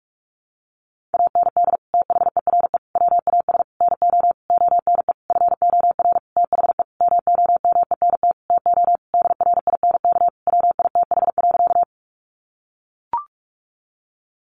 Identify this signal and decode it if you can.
Morse code, heard as a keyed tone, about 44 wpm, 700 Hz. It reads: AND THERE WAS NO ONE FOR THE MOMENT TO DRINK WITH.
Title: Sentences from Top 1500 Words 45wpm